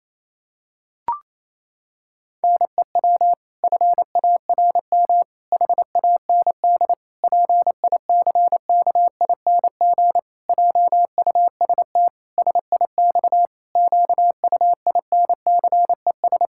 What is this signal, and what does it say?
Morse code, heard as a keyed tone, about 28 wpm, 700 Hz. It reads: NEW FARM HAND PICKING JUST SIX QUINCES
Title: Pangram Challenge v2 - 28wpm